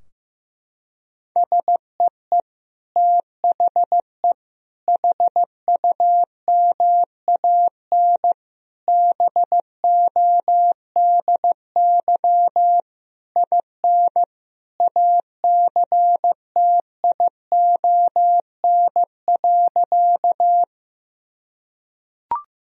Morse code, heard as a keyed tone, about 16 wpm, 700 Hz. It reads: SEE THE HUMAN BODY IN ACTION.
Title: Sentences from Top 1000 Words - Review Part 4 - 15wpm